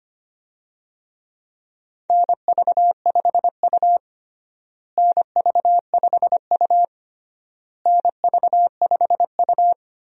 Morse code, heard as a keyed tone, about 25 wpm, 700 Hz. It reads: NV5U NV5U NV5U
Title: USA Call Signs - 4 Characters - Repeated 3x - 25wpm